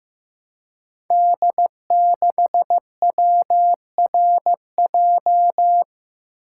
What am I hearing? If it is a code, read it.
Morse code, heard as a keyed tone, about 15 wpm, 700 Hz. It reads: D6WRJ
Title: Code Groups - 5 Letter:Numbers - 15wpm